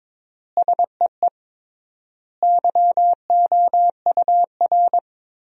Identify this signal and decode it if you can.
Morse code, heard as a keyed tone, about 21 wpm, 700 Hz. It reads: SEE YOUR
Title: 3x Word-Spacing - Sentences from Top 500 Words - 22wpm